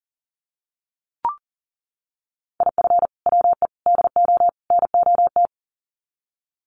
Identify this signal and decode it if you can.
Morse code, heard as a keyed tone, about 40 wpm, 700 Hz. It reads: IF WE DO NOT
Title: Sets of 4 Words 40wpm